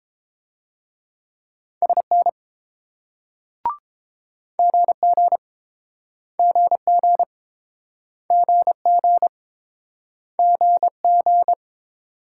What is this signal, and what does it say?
Morse code, heard as a keyed tone, about 27 wpm, 700 Hz. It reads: SN GG GG GG GG
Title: Speed-Racing Multi-letter Phonemes - 22wpm